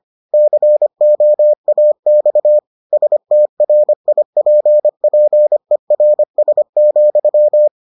Morse code, heard as a keyed tone, about 25 wpm, 600 Hz. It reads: COAX STRIPPERS,